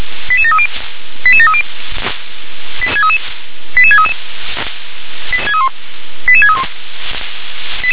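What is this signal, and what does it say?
Five-tone selective call (ZVEI), heard as quick runs of five tones, tones of about 70 ms. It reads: 88730 88730 88633 88631
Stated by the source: ZVEI five-tone selective call 88085 received by SDR.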